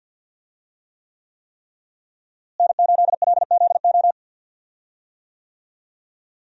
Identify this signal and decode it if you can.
Morse code, heard as a keyed tone, about 52 wpm, 700 Hz. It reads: N8LZO
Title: USA Call Signs - 5 Characters - Repeated 3x - 50wpm